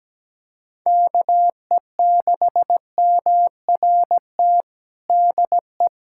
Morse code, heard as a keyed tone, about 17 wpm, 700 Hz. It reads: KE6MRT DE
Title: USA Callsign DE Callsign - 6 Characters - 17wpm